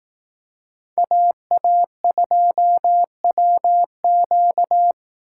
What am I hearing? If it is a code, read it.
Morse code, heard as a keyed tone, about 18 wpm, 700 Hz. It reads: AA2WQ
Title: Speed Racing - USA 2x2 Call Signs 15wpm